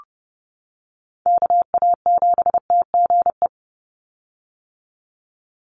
Morse code, heard as a keyed tone, about 30 wpm, 700 Hz. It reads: KA7TGE
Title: USA Call Signs 30wpm